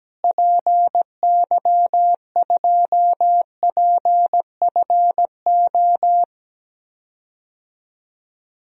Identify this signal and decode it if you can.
Morse code, heard as a keyed tone, about 17 wpm, 700 Hz. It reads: PY2PFO
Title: DXCC-100 Call Signs - 6 Characters - Repeated 2x - 17wpm